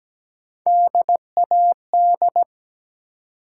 Morse code, heard as a keyed tone, about 17 wpm, 700 Hz. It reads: DAD
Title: Three Letter Words - 17wpm